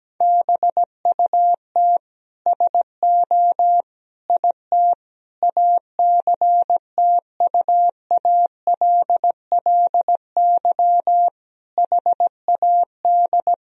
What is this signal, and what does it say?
Morse code, heard as a keyed tone, about 17 wpm, 700 Hz. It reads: BUT SO IT ACTUALLY HAD
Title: Sentences from Top 1000 Words - Review Part 4 - 17wpm